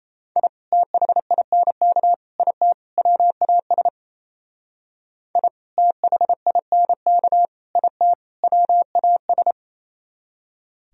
Morse code, heard as a keyed tone, about 36 wpm, 700 Hz. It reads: I THINK IT WAS I THINK IT WAS